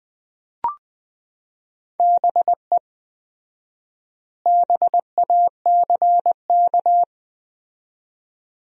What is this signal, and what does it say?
Morse code, heard as a keyed tone, about 19 wpm, 700 Hz. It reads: BE BACK
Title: 4x Word-Spacing - Sentences from Top 500 Words - 20wpm